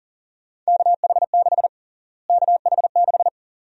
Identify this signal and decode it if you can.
Morse code, heard as a keyed tone, about 40 wpm, 700 Hz. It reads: KH6 KH6
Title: USA Call Sign Prefixes - Repeated 2x - 40wpm